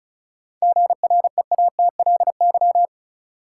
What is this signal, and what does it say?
Morse code, heard as a keyed tone, about 36 wpm, 700 Hz. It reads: GREATLY
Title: Sentences from Top 1400 Words - New Word Drill - 35wpm